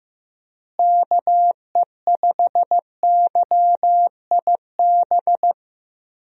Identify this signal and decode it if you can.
Morse code, heard as a keyed tone, about 15 wpm, 700 Hz. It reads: KE5YIB
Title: USA Call Signs - 6 Characters - Repeated 3x - 15wpm